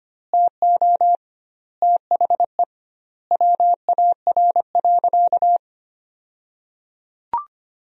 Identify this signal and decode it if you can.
Morse code, heard as a keyed tone, about 25 wpm, 700 Hz. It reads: TO THE WAR.
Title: 2x Word-Spacing - Sentences from Top 400 Words - 25wpm